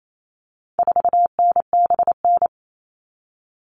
Morse code, heard as a keyed tone, about 28 wpm, 700 Hz. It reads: 4NBN
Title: All Cut Numbers - 4-Digit 28wpm